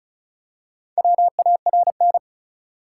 Morse code, heard as a keyed tone, about 36 wpm, 700 Hz. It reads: WARN
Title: Four Letter Words - 35wpm